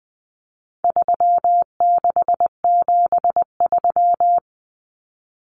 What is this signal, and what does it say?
Morse code, heard as a keyed tone, about 20 wpm, 700 Hz. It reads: 3673
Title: Common Cut Numbers - 4-Digit 20wpm